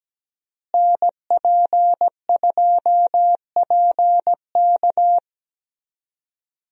Morse code, heard as a keyed tone, about 17 wpm, 700 Hz. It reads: NP2PK
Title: USA Call Signs - 5 Characters - Repeated 2x - 17wpm